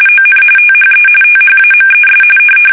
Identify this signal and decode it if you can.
RTTY, about 45.45 baud. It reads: ON THE THAMES
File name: fsk850 / usnavy1